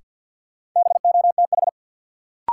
Morse code, heard as a keyed tone, about 52 wpm, 700 Hz. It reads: BOTH
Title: Sentences from Top 300 Words - New Word Drill - 50wpm